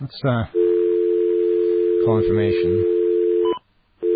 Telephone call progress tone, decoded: dial tone